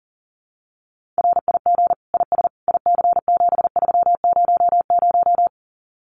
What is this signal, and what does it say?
Morse code, heard as a keyed tone, about 40 wpm, 700 Hz. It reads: RIG IS IC7300
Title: Common QSO Phrases Encore 40wpm